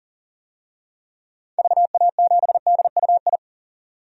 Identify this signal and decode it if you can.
Morse code, heard as a keyed tone, about 40 wpm, 700 Hz. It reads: VA7DUI
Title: DXCC-100 Call Signs - 6 Characters - 40wpm